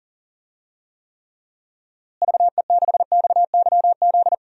Morse code, heard as a keyed tone, about 40 wpm, 700 Hz. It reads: VE6XYZ
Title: DXCC-100 Call Signs - 6 Characters - Repeated 3x - 40wpm